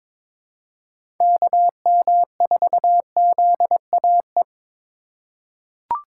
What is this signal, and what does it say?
Morse code, heard as a keyed tone, about 22 wpm, 700 Hz. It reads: KM4ZAE
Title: USA Call Signs - 6 Characters - 4x Character Spacing - 22wpm